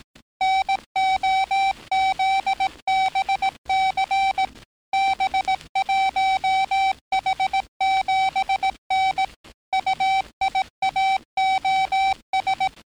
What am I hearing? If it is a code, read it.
Morse code, heard as a keyed tone, about 18 wpm, 770 Hz. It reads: NOZBC B1H7N UIAOS
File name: training / NAVPERS 11049to11060 / NAVPERS 11058 RA RB